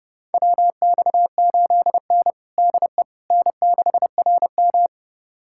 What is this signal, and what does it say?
Morse code, heard as a keyed tone, about 30 wpm, 700 Hz. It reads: WX8N DE N6RM
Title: USA Callsign DE Callsign - 4 Characters - 30wpm